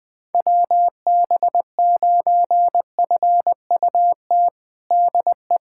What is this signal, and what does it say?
Morse code, heard as a keyed tone, about 20 wpm, 700 Hz. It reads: WB9FUT DE
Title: USA Callsign DE Callsign - 6 Characters - 20wpm